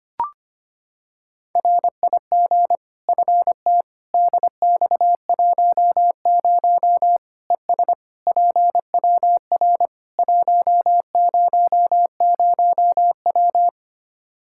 Morse code, heard as a keyed tone, about 25 wpm, 700 Hz. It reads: RIG FT DX10 ES PWR 100W
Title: K1USN SST QSO Practice 25wpm